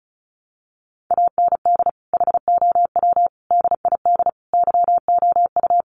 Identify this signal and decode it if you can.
Morse code, heard as a keyed tone, about 34 wpm, 700 Hz. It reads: AND HOW DID YOU
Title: Sentences from Top 200 Words 35wpm